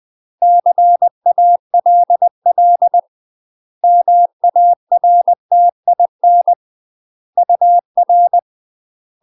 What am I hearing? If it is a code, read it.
Morse code, heard as a keyed tone, about 20 wpm, 700 Hz. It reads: CALL MARTIN UR
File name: UK MASTER QSOs TRAINING / 20WPM / UK MASTER QSOs 0001